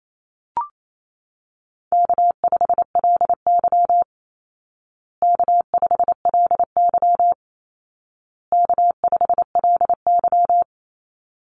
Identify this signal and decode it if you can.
Morse code, heard as a keyed tone, about 28 wpm, 700 Hz. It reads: K5LY K5LY K5LY
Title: Call Signs Repeated 2x - 28wpm